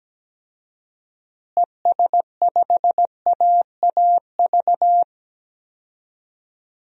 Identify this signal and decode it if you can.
Morse code, heard as a keyed tone, about 17 wpm, 700 Hz. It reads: ES5AAV
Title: DXCC-100 Call Signs - 6 Characters - Repeated 3x - 17wpm